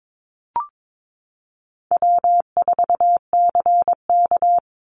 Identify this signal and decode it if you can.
Morse code, heard as a keyed tone, about 22 wpm, 700 Hz. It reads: W4CK
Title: Easy Call Signs Repeated 2x - 22wpm